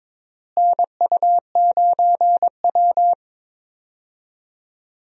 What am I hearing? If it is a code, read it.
Morse code, heard as a keyed tone, about 22 wpm, 700 Hz. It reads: NU9W
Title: USA Call Signs - 4 Characters - Repeated 3x - 22wpm